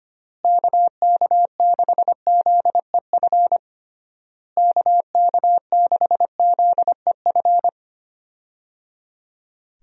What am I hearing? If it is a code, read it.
Morse code, heard as a keyed tone, about 25 wpm, 700 Hz. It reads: KK6ZEF KK6ZEF
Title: USA Call Signs - 6 Characters - Repeated 3x - 25wpm